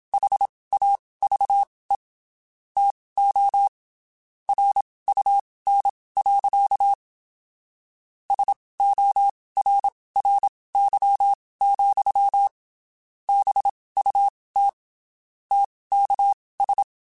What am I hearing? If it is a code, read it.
Morse code, heard as a keyed tone, about 27 wpm, spaced as for 19 wpm, 800 Hz. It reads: HAVE TO RUN. SORRY, BUT TKS
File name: (spaced) / QSOs Spaced-mp3s / kc7sac @25 sp